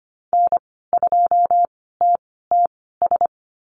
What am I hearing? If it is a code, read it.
Morse code, heard as a keyed tone, about 25 wpm, spaced as for 16 wpm, 700 Hz. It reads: N2TTS
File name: Lesson 15g - Number 5 - Call Signs 25wpm